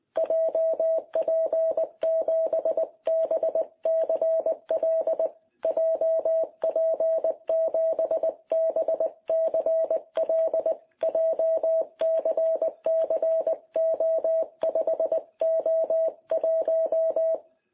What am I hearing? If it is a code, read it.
Morse code, heard as a keyed tone, about 19 wpm, 660 Hz. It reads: JP7BCL JP7BCL JCCO5O1
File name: jh8cbh / 2021cw